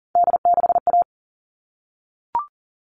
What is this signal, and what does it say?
Morse code, heard as a keyed tone, about 40 wpm, 700 Hz. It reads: D6A
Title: All Cut Numbers - 3-Digit 40wpm